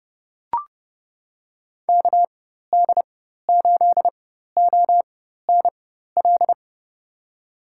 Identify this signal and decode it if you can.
Morse code, heard as a keyed tone, about 31 wpm, spaced as for 14 wpm, 700 Hz. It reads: KD8ONL
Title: USA Call Signs - 4x Character Spacing - 30wpm